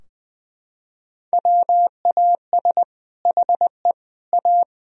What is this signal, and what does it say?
Morse code, heard as a keyed tone, about 21 wpm, 700 Hz. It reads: WAS HE A
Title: Sentences from Top 1000 Words - Review Part 4 - 20wpm